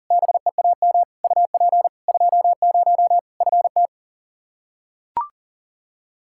Morse code, heard as a keyed tone, about 37 wpm, 700 Hz. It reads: BEAM UP 20 FT
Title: Practice QSO - Exchange 2 - Info - 40wpm